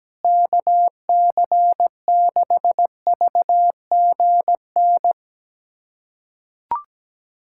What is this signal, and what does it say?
Morse code, heard as a keyed tone, about 17 wpm, 700 Hz. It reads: KC6VGN
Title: USA Callsign DE Callsign 17wpm